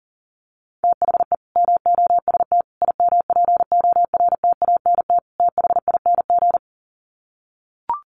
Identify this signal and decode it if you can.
Morse code, heard as a keyed tone, about 38 wpm, 700 Hz. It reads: THE MOST IMPORTANT THING
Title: Sets of 4 Words 40wpm